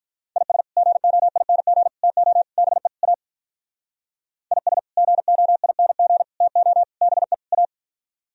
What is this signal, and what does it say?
Morse code, heard as a keyed tone, about 46 wpm, 700 Hz. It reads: IS GOING TO BE A IS GOING TO BE A